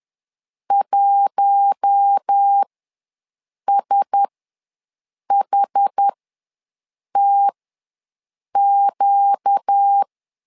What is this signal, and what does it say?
Morse code, heard as a keyed tone, about 10 wpm, spaced as for 6 wpm, 800 Hz. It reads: 1SHTQ
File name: LEZIONI / 4 Q1S ws 10wpm 2208